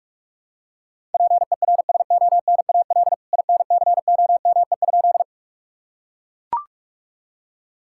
Morse code, heard as a keyed tone, about 43 wpm, 700 Hz. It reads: PERSONAL INCOME?
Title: Sentences from Top 1000 Words - Review Part 5 - 45wpm